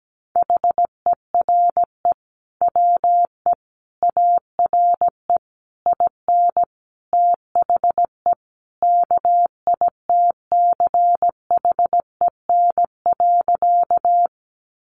Morse code, heard as a keyed tone, about 17 wpm, 700 Hz. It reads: HERE WE ARE IN THE KITCHEN.
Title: Sentences from Top 1400 Words 17wpm